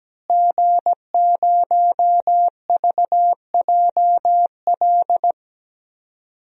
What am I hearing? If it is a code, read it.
Morse code, heard as a keyed tone, about 17 wpm, 700 Hz. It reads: G0VJL
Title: DXCC-100 Call Signs - 5 Characters - Repeated 3x - 17wpm